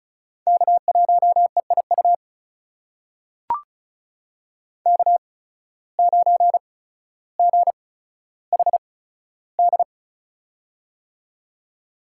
Morse code, heard as a keyed tone, about 34 wpm, spaced as for 15 wpm, 700 Hz. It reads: K1EIU K9GHD
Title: USA Call Signs - 5 Characters - 8x Character Spacing - 35wpm